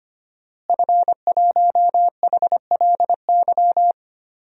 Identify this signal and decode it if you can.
Morse code, heard as a keyed tone, about 25 wpm, 700 Hz. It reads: F1HLY
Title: DXCC-100 Call Signs - 5 Characters - Repeated 3x - 25wpm